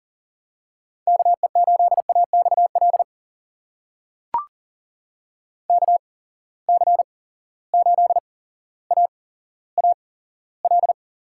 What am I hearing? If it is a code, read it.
Morse code, heard as a keyed tone, about 40 wpm, spaced as for 17 wpm, 700 Hz. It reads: KE8AXL KC8AAL
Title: USA Call Signs - 6 Characters - 8x Character Spacing - 40wpm